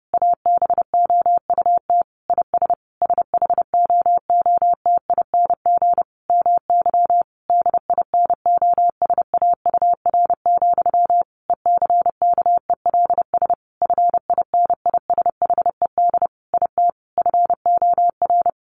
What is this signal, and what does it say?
Morse code, heard as a keyed tone, about 30 wpm, 700 Hz. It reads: ABOUT IS SHOOTING MY DINOSAUR, ECKELS FINISHED IT FOR
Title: A Sound Of Thunder 30wpm